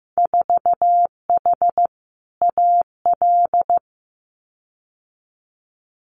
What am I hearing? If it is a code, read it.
Morse code, heard as a keyed tone, about 15 wpm, 700 Hz. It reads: 4H AL
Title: Winter Field Day Exchanges Encore 15wpm